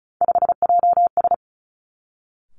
Morse code, heard as a keyed tone, about 36 wpm, 700 Hz. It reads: 5JS
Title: Three Challenging Letter-Number Combinations 35wpm